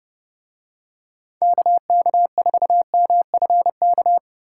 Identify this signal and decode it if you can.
Morse code, heard as a keyed tone, about 28 wpm, 700 Hz. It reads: KK4MFK